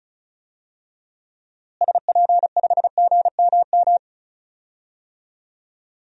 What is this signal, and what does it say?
Morse code, heard as a keyed tone, about 34 wpm, 700 Hz. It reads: SP5GMM
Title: DXCC-100 Call Signs - 6 Characters - Repeated 2x - 35wpm